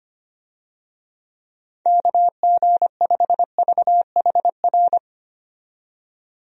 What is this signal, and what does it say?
Morse code, heard as a keyed tone, about 25 wpm, 700 Hz. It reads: KG5VHR
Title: USA Call Signs - 6 Characters - 25wpm